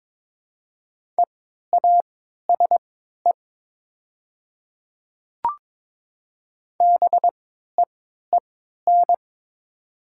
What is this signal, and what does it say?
Morse code, heard as a keyed tone, about 22 wpm, spaced as for 12 wpm, 700 Hz. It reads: EASE BEEN
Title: 3x Character Spacing - Top 500 Words - 22wpm